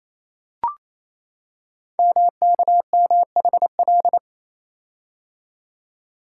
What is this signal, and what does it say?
Morse code, heard as a keyed tone, about 27 wpm, 700 Hz. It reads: MKMHL
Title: Code Groups - 5 Letters - 28wpm